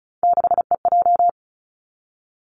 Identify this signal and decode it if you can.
Morse code, heard as a keyed tone, about 36 wpm, 700 Hz. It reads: 6EJ
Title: Three Challenging Letter-Number Combinations 35wpm